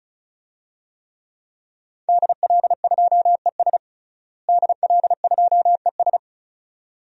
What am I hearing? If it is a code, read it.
Morse code, heard as a keyed tone, about 36 wpm, 700 Hz. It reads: DL2ES DL2ES
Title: DXCC-100 Call Signs - 5 Characters - Repeated 2x - 35wpm